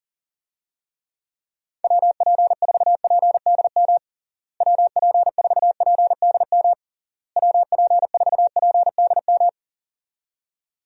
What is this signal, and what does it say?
Morse code, heard as a keyed tone, about 40 wpm, 700 Hz. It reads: WP4PDM WP4PDM WP4PDM
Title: USA Call Signs - 6 Characters - Repeated 3x - 40wpm